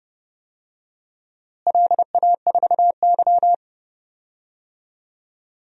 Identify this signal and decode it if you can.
Morse code, heard as a keyed tone, about 30 wpm, 700 Hz. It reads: LA4Y
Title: DXCC-100 Call Signs - 4 Characters - Repeated 3x - 30wpm